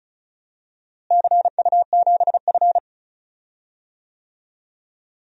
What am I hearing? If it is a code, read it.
Morse code, heard as a keyed tone, about 34 wpm, 700 Hz. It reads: CU7F
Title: DXCC-100 Call Signs - 4 Characters - Repeated 2x - 35wpm